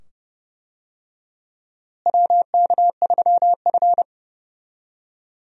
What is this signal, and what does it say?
Morse code, heard as a keyed tone, about 29 wpm, 700 Hz. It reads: WK3F
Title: USA Call Signs - 4 Characters - Repeated 2x - 30wpm